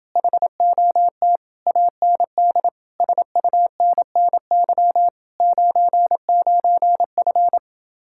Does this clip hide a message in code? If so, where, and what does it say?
Yes: Morse code from 0.2 to 7.6 s, heard as a keyed tone, about 28 wpm, 700 Hz. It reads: HOT AND SUNNY 99F